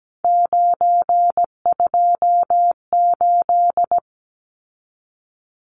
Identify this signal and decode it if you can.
Morse code, heard as a keyed tone, about 17 wpm, 700 Hz. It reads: 928
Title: Numbers - 3-Digit 17wpm